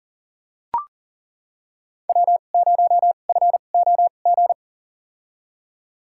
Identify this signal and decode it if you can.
Morse code, heard as a keyed tone, about 40 wpm, spaced as for 29 wpm, 700 Hz. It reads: W0FOG
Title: USA Call Signs - 5 Characters - 2x Character Spacing - 40wpm